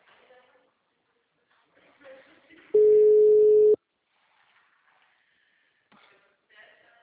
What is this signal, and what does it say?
Ringback tone (Europe, 425 Hz), with speech.